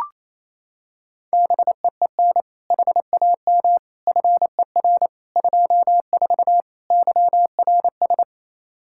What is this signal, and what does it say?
Morse code, heard as a keyed tone, about 27 wpm, 700 Hz. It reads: BEEN HAM FER 24 YRS
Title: Practice QSO - Exchange 3 - More Info - 28wpm